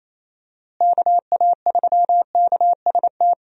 Morse code, heard as a keyed tone, about 28 wpm, 700 Hz. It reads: KA3KST
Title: USA Call Signs - 6 Characters - Repeated 3x - 28wpm